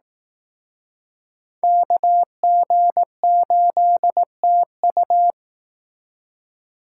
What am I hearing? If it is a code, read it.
Morse code, heard as a keyed tone, about 19 wpm, 700 Hz. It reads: KG8TU
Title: Speed Racing - USA 2x2 Call Signs 15wpm